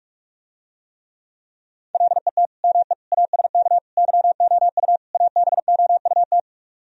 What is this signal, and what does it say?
Morse code, heard as a keyed tone, about 49 wpm, 700 Hz. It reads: LET ME ASK YOU ABOUT